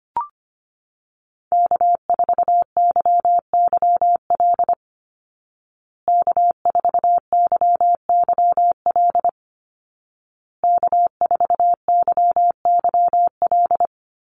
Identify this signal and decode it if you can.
Morse code, heard as a keyed tone, about 25 wpm, 700 Hz. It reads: K4YYL K4YYL K4YYL
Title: Call Signs Repeated 2x - 25wpm